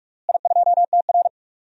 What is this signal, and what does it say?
Morse code, heard as a keyed tone, about 46 wpm, 700 Hz. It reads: I2TR
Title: DXCC-100 Call Signs - 4 Characters - Repeated 3x - 45wpm